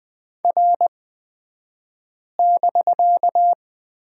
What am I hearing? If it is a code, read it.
Morse code, heard as a keyed tone, about 20 wpm, spaced as for 5 wpm, 700 Hz. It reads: R<BK>
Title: 249 Lesson 40d - BK - Two Characters 20wpm